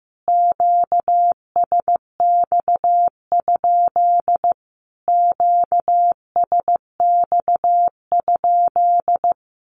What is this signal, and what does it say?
Morse code, heard as a keyed tone, about 15 wpm, 700 Hz. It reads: QSX? QSX?
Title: Q-Codes Repeated 3x 15wpm